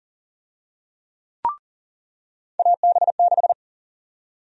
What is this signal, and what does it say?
Morse code, heard as a keyed tone, about 40 wpm, 700 Hz. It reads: AB6
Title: USA Call Sign Prefixes - 40wpm